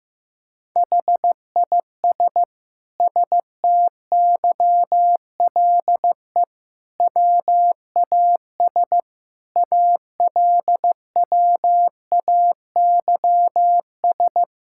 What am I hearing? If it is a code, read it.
Morse code, heard as a keyed tone, about 15 wpm, 700 Hz. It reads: HIS STYLE WAS ALWAYS
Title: Sentences from Top 1000 Words - Review Part 5 - 15wpm